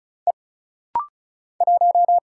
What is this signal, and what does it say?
Morse code, heard as a keyed tone, about 36 wpm, 700 Hz. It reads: E 1
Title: Single Letter-Number - Rapid-Fire 35wpm